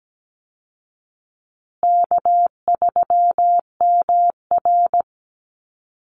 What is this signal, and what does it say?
Morse code, heard as a keyed tone, about 17 wpm, 700 Hz. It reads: K3MR
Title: Call Signs Repeated 3x - 17wpm